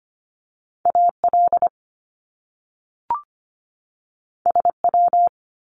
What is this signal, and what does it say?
Morse code, heard as a keyed tone, about 25 wpm, 700 Hz. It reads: AL SW
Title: Top Two Letter Combinations in Words 25wpm